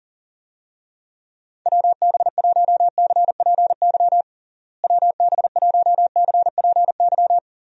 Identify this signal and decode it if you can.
Morse code, heard as a keyed tone, about 40 wpm, 700 Hz. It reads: WB1CPY WB1CPY
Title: USA Call Signs - 6 Characters - Repeated 2x - 40wpm